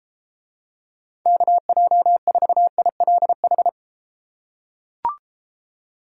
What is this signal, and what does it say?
Morse code, heard as a keyed tone, about 34 wpm, 700 Hz. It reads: KJ4ILH